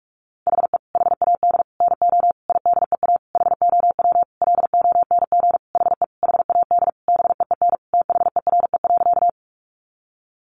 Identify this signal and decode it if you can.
Morse code, heard as a keyed tone, about 44 wpm, 700 Hz. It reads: HE HAD NO IDEA HOW LONG HE HAD BEEN THERE.
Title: Sentences from Top 300 Words 45wpm